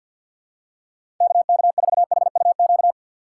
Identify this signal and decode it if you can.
Morse code, heard as a keyed tone, about 52 wpm, 700 Hz. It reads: KK4HUQ
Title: USA Call Signs - 6 Characters - 6x Character Spacing - 50wpm